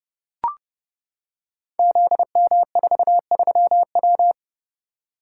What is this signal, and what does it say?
Morse code, heard as a keyed tone, about 28 wpm, 700 Hz. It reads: ZM43W
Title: Code Groups - 5 Letter:Numbers - 30wpm